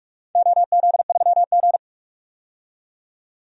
Morse code, heard as a keyed tone, about 46 wpm, 700 Hz. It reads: OZ3G
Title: DXCC-100 Call Signs - 4 Characters - 45wpm